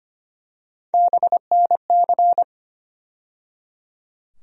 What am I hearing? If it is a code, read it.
Morse code, heard as a keyed tone, about 25 wpm, 700 Hz. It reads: BNC
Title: USA Call Sign Postfixes - 3 Letters 25wpm